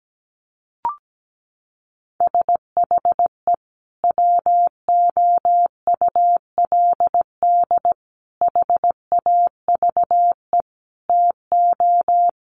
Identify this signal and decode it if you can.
Morse code, heard as a keyed tone, about 17 wpm, 700 Hz. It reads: SHE WOULD HAVE TO
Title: Sentences from Top 1200 Words 17wpm